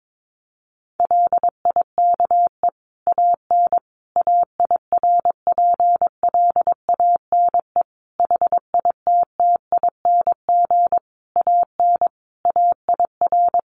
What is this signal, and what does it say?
Morse code, heard as a keyed tone, about 22 wpm, 700 Hz. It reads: LIKE AN AIRPLANE HITTING AN AIR
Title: A Sound Of Thunder 22wpm